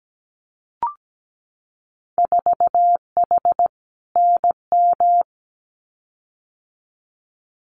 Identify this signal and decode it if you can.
Morse code, heard as a keyed tone, about 17 wpm, 700 Hz. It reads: 4H NM
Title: Winter Field Day Exchanges Encore 17wpm